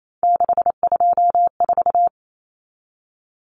Morse code, heard as a keyed tone, about 28 wpm, 700 Hz. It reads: 624
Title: Numbers - 3-Digit 28wpm